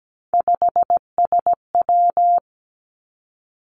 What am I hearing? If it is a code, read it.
Morse code, heard as a keyed tone, about 17 wpm, 700 Hz. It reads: 5SW
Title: Three Challenging Letter-Number Combinations 17wpm